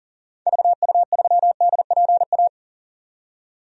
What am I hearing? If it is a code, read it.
Morse code, heard as a keyed tone, about 40 wpm, 700 Hz. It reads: VU3DPA
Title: DXCC-100 Call Signs - 6 Characters - 40wpm